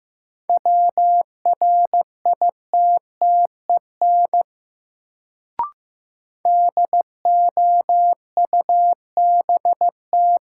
Morse code, heard as a keyed tone, about 15 wpm, 700 Hz. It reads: WRITTEN DOUBT
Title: Sentences from Top 1500 Words - New Word Drill - 15wpm